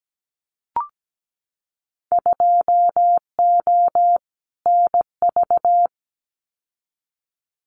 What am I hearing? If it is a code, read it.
Morse code, heard as a keyed tone, about 17 wpm, 690 Hz. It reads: 2O NV
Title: Winter Field Day Exchanges 17wpm